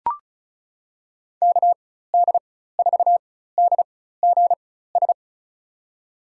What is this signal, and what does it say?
Morse code, heard as a keyed tone, about 34 wpm, spaced as for 16 wpm, 690 Hz. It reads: KD4DGS
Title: USA Call Signs - 4x Character Spacing - 35wpm